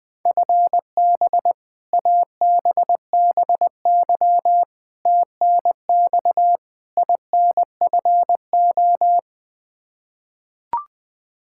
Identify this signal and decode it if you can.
Morse code, heard as a keyed tone, about 21 wpm, 700 Hz. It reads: FB ABBY TNX INFO
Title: Practice QSO - Exchange 3 - More Info - 20wpm